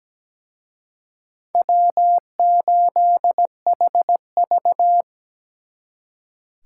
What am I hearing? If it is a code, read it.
Morse code, heard as a keyed tone, about 20 wpm, 700 Hz. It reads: W8HV
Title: Speed Racing - USA Call Signs 17wpm